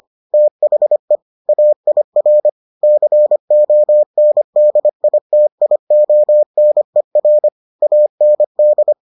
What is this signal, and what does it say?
Morse code, heard as a keyed tone, about 25 wpm, 600 Hz. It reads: THE AIR CONDITIONER AND